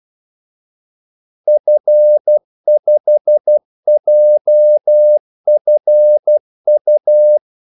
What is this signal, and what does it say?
Morse code, heard as a keyed tone, about 12 wpm, 600 Hz. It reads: F5JFU
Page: 2 ekstra ordmellemrum mellem kaldesignalerne